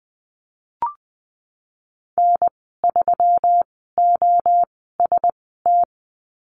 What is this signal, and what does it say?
Morse code, heard as a keyed tone, about 20 wpm, spaced as for 15 wpm, 700 Hz. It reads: N3OST
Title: 122 Lesson 20g - Number 3 - Call Signs 20wpm